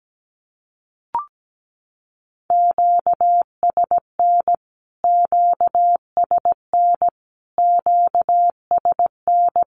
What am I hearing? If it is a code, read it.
Morse code, heard as a keyed tone, about 17 wpm, 700 Hz. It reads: QSN QSN QSN
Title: Q-Codes Repeated 3x 17wpm